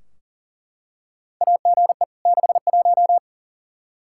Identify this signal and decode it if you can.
Morse code, heard as a keyed tone, about 40 wpm, 700 Hz. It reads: AGE 61
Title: K1USN SST QSO Practice 40wpm